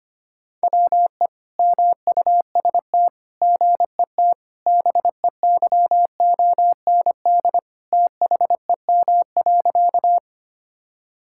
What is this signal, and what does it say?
Morse code, heard as a keyed tone, about 25 wpm, 700 Hz. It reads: WE MUST GET BEYOND THEM.
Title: Sentences from Top 1000 Words - Review Part 6 - 25wpm